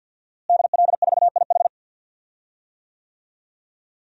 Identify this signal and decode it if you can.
Morse code, heard as a keyed tone, about 51 wpm, 700 Hz. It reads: DL4IH
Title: DXCC-100 Call Signs - 5 Characters - Repeated 3x - 50wpm